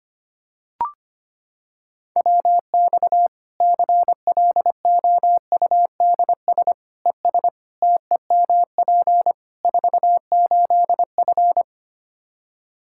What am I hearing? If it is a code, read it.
Morse code, heard as a keyed tone, about 25 wpm, 700 Hz. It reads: WX CLOUDS ES TEMP 48F
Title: Practice QSO - Exchange 2 - Info - 25wpm